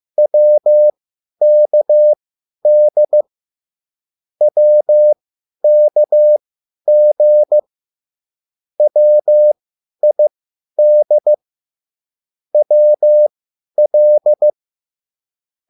Morse code, heard as a keyed tone, about 15 wpm, spaced as for 10 wpm, 600 Hz. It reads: WKD WKG WID WL
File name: QSL / CW lesson